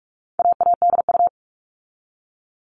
Morse code, heard as a keyed tone, about 44 wpm, 700 Hz. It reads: AADU
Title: All Cut Numbers - 4-Digit 45wpm